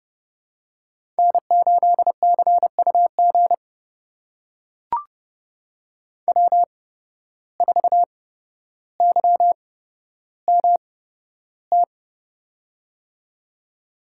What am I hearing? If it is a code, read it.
Morse code, heard as a keyed tone, about 30 wpm, spaced as for 13 wpm, 700 Hz. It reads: N8CUG W4YMT
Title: USA Call Signs - 5 Characters - 8x Character Spacing - 30wpm